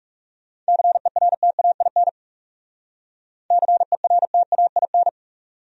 Morse code, heard as a keyed tone, about 39 wpm, 700 Hz. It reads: CERTAIN CERTAIN